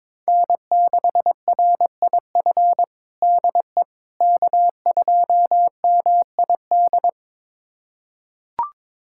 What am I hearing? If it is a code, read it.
Morse code, heard as a keyed tone, about 21 wpm, 700 Hz. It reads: N6RIF DE K2MID
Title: USA Callsign DE Callsign - 5 Characters - 22wpm